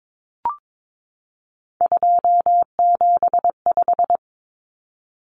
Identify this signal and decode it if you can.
Morse code, heard as a keyed tone, about 22 wpm, 700 Hz. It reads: 275
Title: Common Cut Numbers - 3-Digit 22wpm